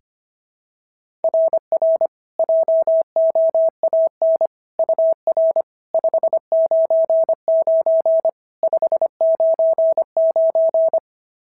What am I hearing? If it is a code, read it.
Morse code, heard as a keyed tone, about 25 wpm, 690 Hz. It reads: RR JOAN UR 599 599
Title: SOTA QSO Practice 25wpm